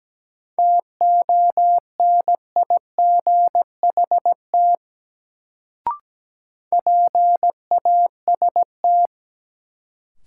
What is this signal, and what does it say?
Morse code, heard as a keyed tone, about 17 wpm, 700 Hz. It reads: TONIGHT PAST
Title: Sentences from Top 600 Words - New Word Drill - 17wpm